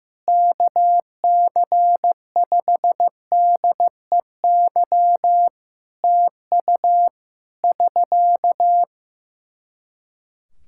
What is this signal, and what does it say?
Morse code, heard as a keyed tone, about 14 wpm, 700 Hz. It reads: KC5DEY TU <SK>
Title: Practice QSO - Exchange 4 - Ending - 15wpm